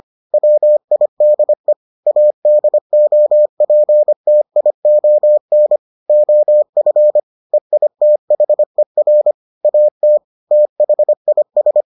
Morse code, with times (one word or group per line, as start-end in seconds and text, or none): 0.3-1.7 WIDE
2.1-5.8 ADOPTION
6.1-7.2 OF
7.5-9.3 EITHER
9.6-10.2 AT
10.5-11.8 THIS